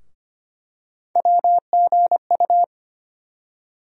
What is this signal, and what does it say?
Morse code, heard as a keyed tone, about 25 wpm, 700 Hz. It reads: WGU
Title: USA Call Sign Postfixes - 3 Letters 25wpm